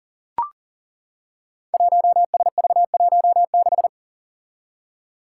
Morse code, heard as a keyed tone, about 38 wpm, 700 Hz. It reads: 1SV16
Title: Five Challenging Letter-Number Combinations 40wpm